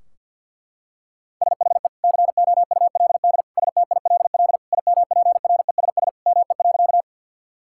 Morse code, heard as a keyed tone, about 50 wpm, 700 Hz. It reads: SHE COULD STILL IMPRESS ME.
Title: Sentences from Top 1600 Words 50wpm